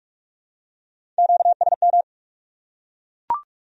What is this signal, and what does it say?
Morse code, heard as a keyed tone, about 44 wpm, 700 Hz. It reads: QSM